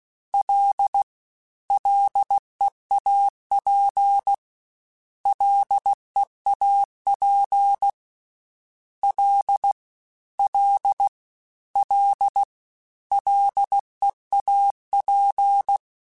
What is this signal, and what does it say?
Morse code, heard as a keyed tone, about 16 wpm, 800 Hz. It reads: L LEAP LEAP L L L LEAP
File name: Combinations / Combinations-mp3s / Combinations@15